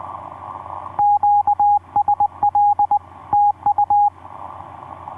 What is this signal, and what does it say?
Morse code, heard as a keyed tone, about 20 wpm, 820 Hz, with radio noise. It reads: QSL TU
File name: v5-dk1ce sq7nsp etc 12m cw 02jan2011 1320